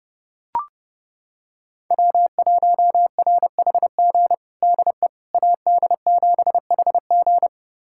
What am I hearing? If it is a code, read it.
Morse code, heard as a keyed tone, about 30 wpm, 700 Hz. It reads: W1RHG DE AD7HG
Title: USA Callsign DE Callsign - 5 Characters - 30wpm